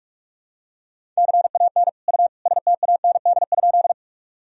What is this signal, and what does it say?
Morse code, heard as a keyed tone, about 43 wpm, 700 Hz. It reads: CAN U STAND?
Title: Sentences from Top 300 Words and Abbreviations 45wpm